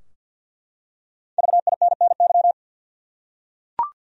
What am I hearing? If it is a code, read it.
Morse code, heard as a keyed tone, about 51 wpm, 700 Hz. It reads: VINNY
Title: USA Common Names - 50wpm